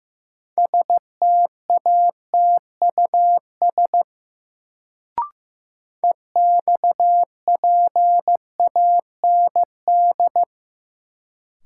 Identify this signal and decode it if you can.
Morse code, heard as a keyed tone, about 15 wpm, 700 Hz. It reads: STATUS EXPAND
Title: Sentences from Top 1400 Words - New Word Drill - 15wpm